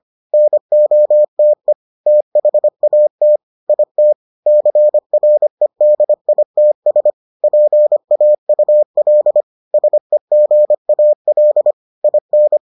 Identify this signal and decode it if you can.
Morse code, heard as a keyed tone, about 25 wpm, 600 Hz. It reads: NOTE THAT IT CREDITS PAUL SEGAL IN